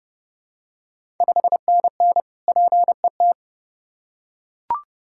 Morse code, heard as a keyed tone, about 30 wpm, 700 Hz. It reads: 5NN PET
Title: Ontario QSO Party Exchange 30wpm